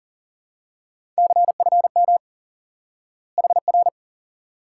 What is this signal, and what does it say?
Morse code, heard as a keyed tone, about 40 wpm, 700 Hz. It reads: CFM HR
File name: IQE Course 3 - Lesson 89 - YAGI - Two Elements - 40wpm